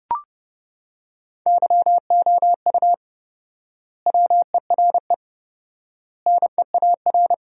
Morse code, heard as a keyed tone, about 30 wpm, 700 Hz. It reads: YOU WERE NEAR
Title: 4x Word-Spacing - Sentences from Top 200 Words - 30wpm